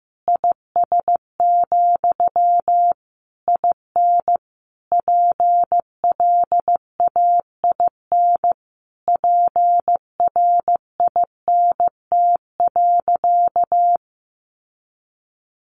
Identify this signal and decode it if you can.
Morse code, heard as a keyed tone, about 14 wpm, 700 Hz. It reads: IS, IN PLAIN PRINT.
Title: Sentences from Top 1400 Words 15wpm